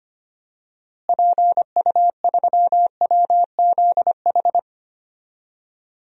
Morse code, heard as a keyed tone, about 25 wpm, 700 Hz. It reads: PU3WZH
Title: DXCC-100 Call Signs - 6 Characters - Repeated 2x - 25wpm